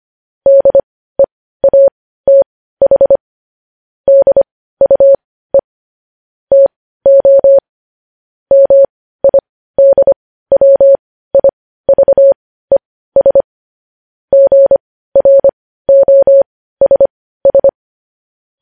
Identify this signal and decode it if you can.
Morse code, heard as a keyed tone, about 25 wpm, spaced as for 15 wpm, 550 Hz. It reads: DEATH DUE TO MIDWIVES GROSS
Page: News Headlines in Morse Code at 15 WPM